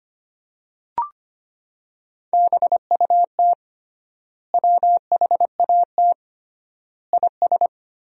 Morse code, heard as a keyed tone, about 25 wpm, 700 Hz. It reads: BUT WHAT IS
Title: 3x Word-Spacing - Sentences from Top 100 Words - 25wpm